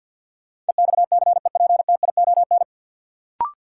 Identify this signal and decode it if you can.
Morse code, heard as a keyed tone, about 52 wpm, 700 Hz. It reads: EXCEPTION
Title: Sets of 4 Words - New Word Drill - 50wpm